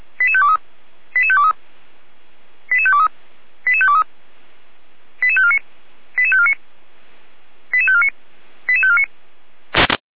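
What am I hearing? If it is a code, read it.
Five-tone selective call (ZVEI), heard as quick runs of five tones, tones of about 70 ms. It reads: 88523 88523 88523 88523 88549 88549 88549 88549
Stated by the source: ZVEI five-tone selective call received on an SDR.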